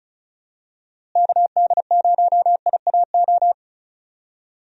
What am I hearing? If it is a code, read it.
Morse code, heard as a keyed tone, about 34 wpm, 700 Hz. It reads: KD0IAO
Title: USA Call Signs - 6 Characters - Repeated 3x - 35wpm